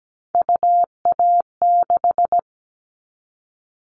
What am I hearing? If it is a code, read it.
Morse code, heard as a keyed tone, about 17 wpm, 700 Hz. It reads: UA6
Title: Callsign Prefixes for Top 100 DXCC - 17wpm